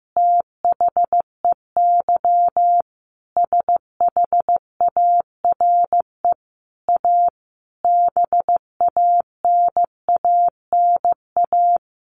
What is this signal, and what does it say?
Morse code, heard as a keyed tone, about 15 wpm, 700 Hz. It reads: THEY SHARE A BANANA
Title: Easy Phrases 15wpm